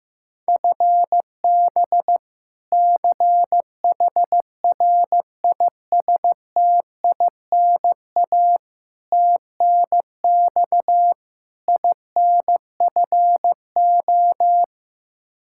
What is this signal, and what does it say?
Morse code, heard as a keyed tone, about 15 wpm, 700 Hz. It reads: FB CHRISTINA TNX INFO
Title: Practice QSO - Exchange 3 - More Info - 15wpm